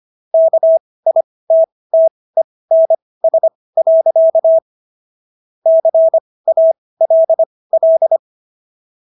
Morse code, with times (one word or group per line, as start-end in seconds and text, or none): 0.3-4.6 KITTENS.
5.7-8.2 CALL